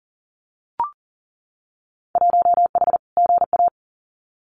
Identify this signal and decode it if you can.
Morse code, heard as a keyed tone, about 40 wpm, 700 Hz. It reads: 1H GA
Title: Winter Field Day Exchanges 40wpm